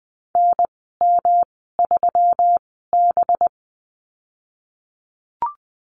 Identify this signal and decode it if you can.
Morse code, heard as a keyed tone, about 20 wpm, spaced as for 15 wpm, 700 Hz. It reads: NM3B
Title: 193 Lesson 31g - Letter v - Call Signs 20wpm